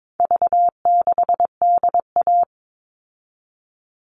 Morse code, heard as a keyed tone, about 22 wpm, 700 Hz. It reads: V6DA
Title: All Cut Numbers - 4-Digit 22wpm